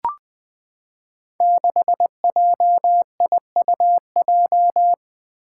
Morse code, heard as a keyed tone, about 20 wpm, 700 Hz. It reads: 6JIUJ
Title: Five Challenging Letter-Number Combinations 20wpm